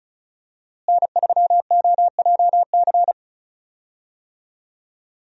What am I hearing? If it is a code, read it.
Morse code, heard as a keyed tone, about 34 wpm, 700 Hz. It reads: N3OJC
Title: USA Call Signs - 5 Characters - Repeated 2x - 35wpm